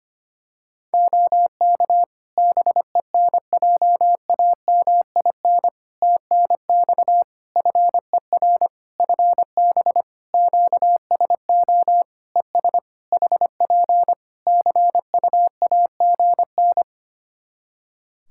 Morse code, heard as a keyed tone, about 25 wpm, 700 Hz. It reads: OK BENJAMIN TNX FER FB QSO ES HP CUAGN
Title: Practice QSO - Exchange 4 - Ending - 25wpm